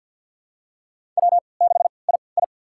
Morse code, heard as a keyed tone, about 47 wpm, spaced as for 28 wpm, 700 Hz. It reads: W6II
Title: USA Call Signs - 3x Character Spacing - 50wpm